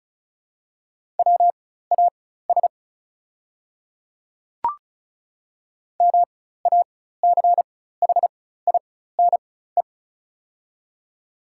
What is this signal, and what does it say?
Morse code, heard as a keyed tone, about 36 wpm, spaced as for 16 wpm, 700 Hz. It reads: WAS MACHINE
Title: 4x Character Spacing - Top 500 Words - 35wpm